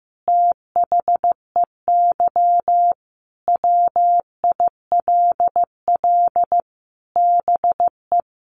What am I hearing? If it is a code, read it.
Morse code, heard as a keyed tone, about 15 wpm, 700 Hz. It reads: THEY WILL BE
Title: Sentences from Top 700 Words 15wpm